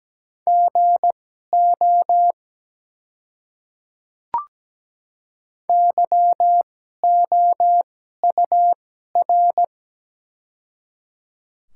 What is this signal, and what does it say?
Morse code, heard as a keyed tone, about 17 wpm, spaced as for 12 wpm, 690 Hz. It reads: GO YOUR
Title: 2x Character Spacing - Top 100 Words - 17wpm